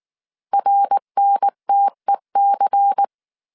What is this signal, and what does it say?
Morse code, heard as a keyed tone, about 19 wpm, 800 Hz. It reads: RNTEC
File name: CW / DET / 1 6 1500C S C 18wpm 2356